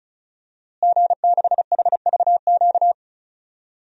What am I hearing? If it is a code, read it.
Morse code, heard as a keyed tone, about 36 wpm, 700 Hz. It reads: G6HVQ
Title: DXCC-100 Call Signs - 5 Characters - Repeated 3x - 35wpm